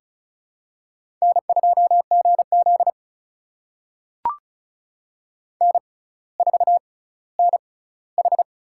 Morse code, heard as a keyed tone, about 36 wpm, spaced as for 18 wpm, 700 Hz. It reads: N2GZ N4NH
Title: USA Call Signs - 4 Characters - 6x Character Spacing - 35wpm